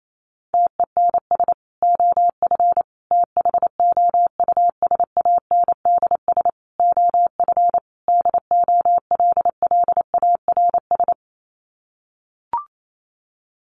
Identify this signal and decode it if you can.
Morse code, heard as a keyed tone, about 28 wpm, 700 Hz. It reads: TENS OF THOUSANDS OF DOLLARS
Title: Sets of 5 Words Encore2 28wpm